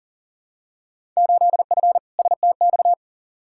Morse code, heard as a keyed tone, about 40 wpm, 700 Hz. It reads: 8F STX
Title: Field Day Exchanges Encore - 40wpm